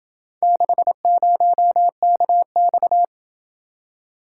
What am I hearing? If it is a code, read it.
Morse code, heard as a keyed tone, about 28 wpm, 700 Hz. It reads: 60KX